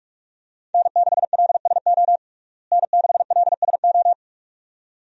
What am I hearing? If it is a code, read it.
Morse code, heard as a keyed tone, about 46 wpm, 700 Hz. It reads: N6LSO N6LSO
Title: USA Call Signs - 5 Characters - Repeated 2x - 45wpm